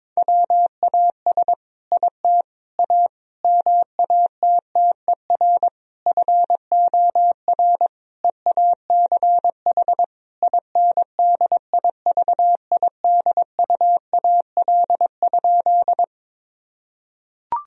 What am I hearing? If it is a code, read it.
Morse code, heard as a keyed tone, about 22 wpm, 700 Hz. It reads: WAS IT A MATTER FOR EACH INDIVIDUAL?
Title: Sentences from Top 1000 Words - Review Part 4 - 22wpm